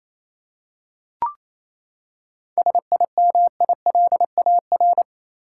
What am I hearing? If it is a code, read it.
Morse code, heard as a keyed tone, about 28 wpm, 700 Hz. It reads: SIMILAR
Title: Seven Letter Words - 28wpm